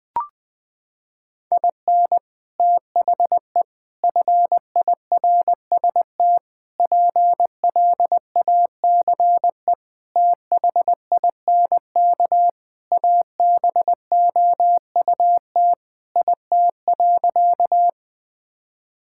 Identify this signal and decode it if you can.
Morse code, heard as a keyed tone, about 20 wpm, 700 Hz. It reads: IN THE FIRST PLACE THINK ABOUT IT.
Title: Sentences with 3-Word Phrases - 20wpm